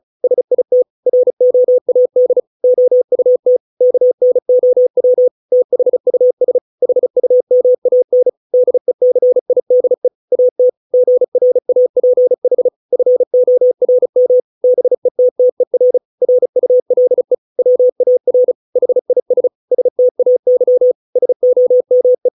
Morse code, heard as a keyed tone, about 34 wpm, 500 Hz. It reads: SIT ROAD OUT KNOW THUS HUMAN DECIDE AT GRAPH FORM BETTER RULE WAR HIS STAY SOME